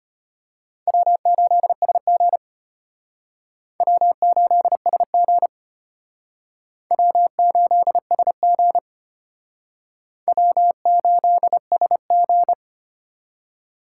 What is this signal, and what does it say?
Morse code, heard as a keyed tone, about 32 wpm, 700 Hz. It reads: W8SG W8SG W8SG W8SG